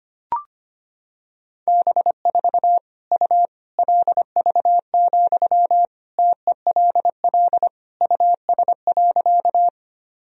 Morse code, heard as a keyed tone, about 25 wpm, 700 Hz. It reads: B4 U LV, TELL US.
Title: Sentences from Top 400 Words and Abbreviations 25wpm